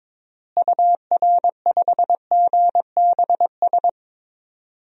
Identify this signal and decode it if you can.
Morse code, heard as a keyed tone, about 22 wpm, 700 Hz. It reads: UR5GBS
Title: DXCC-100 Call Signs - 6 Characters - Repeated 2x - 22wpm